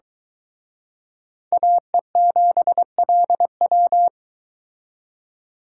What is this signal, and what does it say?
Morse code, heard as a keyed tone, about 23 wpm, 700 Hz. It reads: AE7LW